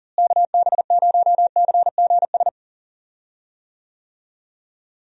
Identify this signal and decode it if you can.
Morse code, heard as a keyed tone, about 39 wpm, 700 Hz. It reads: KB0CGS
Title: USA Call Signs - 6 Characters - Repeated 2x - 40wpm